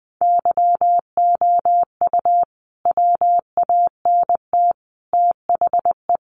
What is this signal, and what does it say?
Morse code, heard as a keyed tone, about 20 wpm, 700 Hz. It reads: YOU WANT THE
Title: A Sound Of Thunder 20wpm